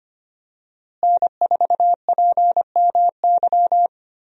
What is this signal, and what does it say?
Morse code, heard as a keyed tone, about 25 wpm, 700 Hz. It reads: N4PMY
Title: USA Call Signs - 5 Characters - Repeated 2x - 25wpm